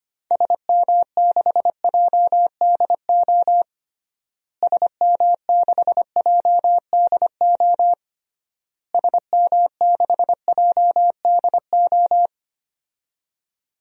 Morse code, heard as a keyed tone, about 25 wpm, 700 Hz. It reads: SM6JDO SM6JDO SM6JDO
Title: DXCC-100 Call Signs - 6 Characters - Repeated 3x - 25wpm